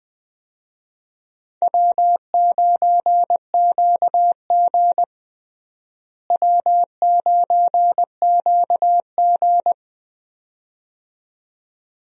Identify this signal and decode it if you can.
Morse code, heard as a keyed tone, about 21 wpm, 700 Hz. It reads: W9QG W9QG
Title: USA Call Signs - 4 Characters - Repeated 2x - 20wpm